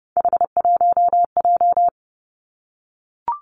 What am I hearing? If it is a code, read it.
Morse code, heard as a keyed tone, about 30 wpm, 700 Hz. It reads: H1J
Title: Three Challenging Letter-Number Combinations 30wpm